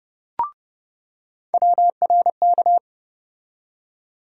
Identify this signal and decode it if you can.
Morse code, heard as a keyed tone, about 30 wpm, 700 Hz. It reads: WRK
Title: USA Call Sign Postfixes - 3 Letters 30wpm